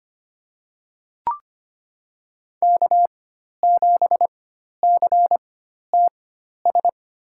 Morse code, heard as a keyed tone, about 25 wpm, spaced as for 12 wpm, 700 Hz. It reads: K7CTS
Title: USA Call Signs - 5 Characters - 4x Character Spacing - 25wpm